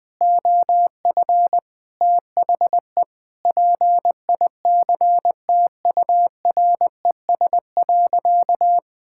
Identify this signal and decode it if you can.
Morse code, heard as a keyed tone, about 20 wpm, 700 Hz. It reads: OF THE PICTURES.
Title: Sentences with 8-Word Phrases - 20wpm